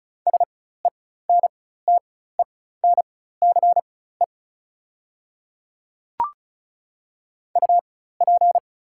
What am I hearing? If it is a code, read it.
Morse code, heard as a keyed tone, about 35 wpm, spaced as for 16 wpm, 690 Hz. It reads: SENTENCE UP
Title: 4x Character Spacing - Top 200 Words - 35wpm